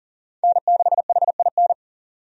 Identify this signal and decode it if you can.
Morse code, heard as a keyed tone, about 40 wpm, 700 Hz. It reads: N6HIN
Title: USA Call Signs - 5 Characters - Repeated 2x - 40wpm